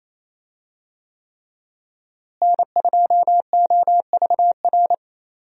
Morse code, heard as a keyed tone, about 28 wpm, 700 Hz. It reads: N2OVR
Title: USA Call Signs - 5 Characters - Repeated 3x - 28wpm